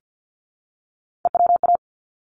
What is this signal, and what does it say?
Morse code, heard as a keyed tone, about 49 wpm, 700 Hz. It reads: EWA
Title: ARRL Sections 50wpm